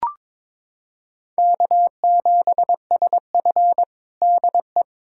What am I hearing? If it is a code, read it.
Morse code, heard as a keyed tone, about 22 wpm, 700 Hz. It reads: K7SF DE
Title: K1USN SST QSO Practice 22wpm